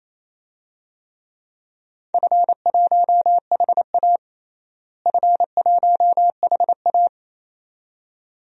Morse code, heard as a keyed tone, about 28 wpm, 700 Hz. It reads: F1HA F1HA
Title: DXCC-100 Call Signs - 4 Characters - Repeated 2x - 28wpm